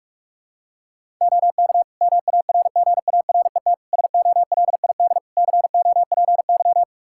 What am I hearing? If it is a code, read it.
Morse code, heard as a keyed tone, about 42 wpm, 700 Hz. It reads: OK MARGARET SOLID COPY
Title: Practice QSO - Exchange 3 - More Info - 45wpm